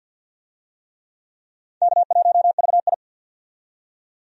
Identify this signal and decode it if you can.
Morse code, heard as a keyed tone, about 48 wpm, 700 Hz. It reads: K1VI
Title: USA Call Signs - 4 Characters - 50wpm